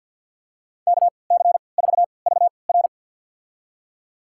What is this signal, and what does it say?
Morse code, heard as a keyed tone, about 53 wpm, spaced as for 29 wpm, 700 Hz. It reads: KC4VR
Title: USA Call Signs - 5 Characters - 3x Character Spacing - 50wpm